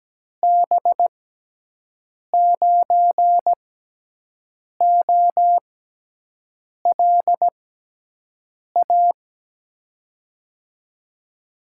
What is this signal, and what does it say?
Morse code, heard as a keyed tone, about 17 wpm, spaced as for 6 wpm, 700 Hz. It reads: B9OLA
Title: USA Call Signs - 6x Character Spacing - 17wpm